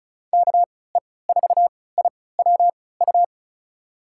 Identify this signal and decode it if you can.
Morse code, heard as a keyed tone, about 34 wpm, spaced as for 20 wpm, 700 Hz. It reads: KE4IWU
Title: USA Call Signs - 3x Character Spacing - 35wpm